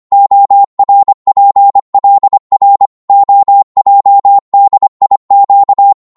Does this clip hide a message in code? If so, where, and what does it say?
Yes: Morse code from 0.1 to 5.9 s, heard as a keyed tone, about 25 wpm, 800 Hz. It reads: ORPLR OJDIQ